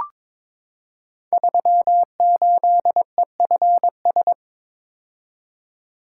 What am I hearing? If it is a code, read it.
Morse code, heard as a keyed tone, about 22 wpm, 700 Hz. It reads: 38EFS
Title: Code Groups - 5 Letter:Numbers - 22wpm